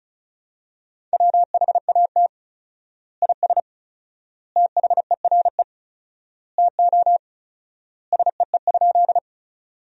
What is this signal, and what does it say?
Morse code, heard as a keyed tone, about 34 wpm, 700 Hz. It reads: WHAT IS THERE TO SEE?
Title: 4x Word-Spacing - Sentences from Top 100 Words - 35wpm